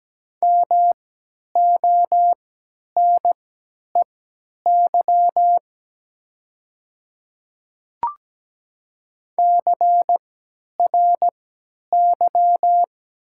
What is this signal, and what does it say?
Morse code, heard as a keyed tone, about 17 wpm, spaced as for 10 wpm, 700 Hz. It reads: MONEY CRY
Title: 3x Character Spacing - Top 500 Words - 17wpm